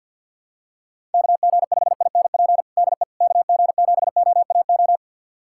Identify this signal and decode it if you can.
Morse code, heard as a keyed tone, about 49 wpm, 700 Hz. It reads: KG5INP DE KG7OAO
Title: USA Callsign DE Callsign - 6 Characters - 50wpm